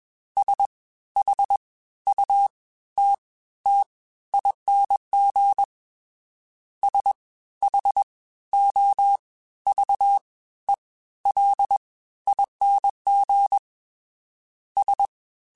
Morse code, heard as a keyed tone, about 20 wpm, spaced as for 13 wpm, 800 Hz. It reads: SHUTTING SHOVELING S